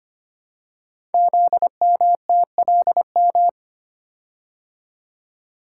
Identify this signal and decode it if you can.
Morse code, heard as a keyed tone, about 25 wpm, 700 Hz. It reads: ZMTLM
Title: Code Groups - 5 Letters - 25wpm